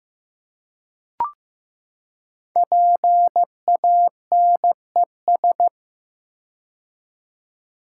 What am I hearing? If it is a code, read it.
Morse code, heard as a keyed tone, about 15 wpm, 700 Hz. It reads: PANES
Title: Code Groups - 5 Letters - 15wpm